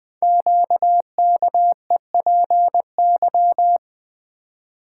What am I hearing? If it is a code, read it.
Morse code, heard as a keyed tone, about 20 wpm, 700 Hz. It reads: QKEPY
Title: Code Groups - 5 Letters - 20wpm